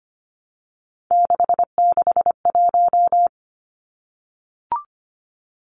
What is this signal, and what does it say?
Morse code, heard as a keyed tone, about 25 wpm, 700 Hz. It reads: 661
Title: Numbers - 3-Digit 25wpm